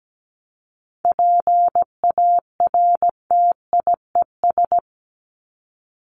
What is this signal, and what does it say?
Morse code, heard as a keyed tone, about 17 wpm, 700 Hz. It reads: PARTIES
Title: ES:S Suffix - Top 100 Words with Base Words 17wpm